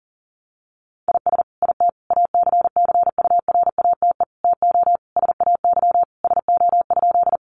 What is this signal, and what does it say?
Morse code, heard as a keyed tone, about 40 wpm, 700 Hz. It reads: IS IT ACCURATE TO SAY SO?
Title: Sentences from Top 1200 Words 40wpm